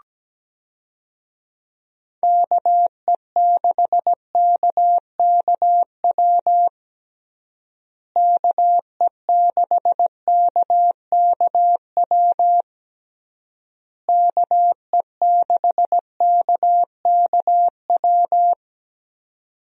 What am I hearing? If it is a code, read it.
Morse code, heard as a keyed tone, about 17 wpm, 700 Hz. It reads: KE6KKW KE6KKW KE6KKW
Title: USA Call Signs - 6 Characters - Repeated 3x - 17wpm